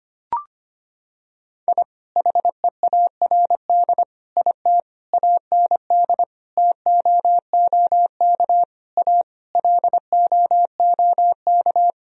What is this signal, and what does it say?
Morse code, heard as a keyed tone, about 25 wpm, 700 Hz. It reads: I HEARD IT AND TOOK A LOOK
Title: Sentences from Top 1000 Words - Review Part 1 - 25wpm